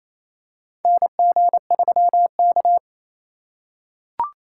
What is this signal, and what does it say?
Morse code, heard as a keyed tone, about 28 wpm, 700 Hz. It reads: NG3K
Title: USA Call Signs - 4 Characters - 4x Character Spacing - 28wpm